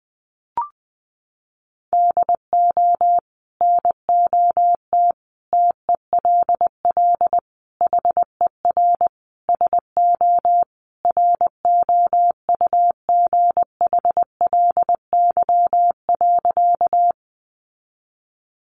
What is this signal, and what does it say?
Morse code, heard as a keyed tone, about 20 wpm, 700 Hz. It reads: DO NOT TELL HER SO ROUGHLY.
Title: Sentences from Top 1500 Words 20wpm